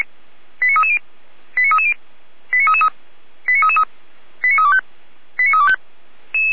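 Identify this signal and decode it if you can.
Five-tone selective call (ZVEI), heard as quick runs of five tones, tones of about 70 ms. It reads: 89330 89330 89333 89333 89325 89325
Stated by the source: ZVEI five-tone call 89501 received with an SDR.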